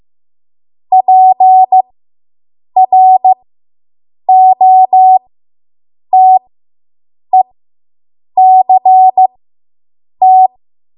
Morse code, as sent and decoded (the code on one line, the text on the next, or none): .--. .-. --- - . -.-. -
PROTECT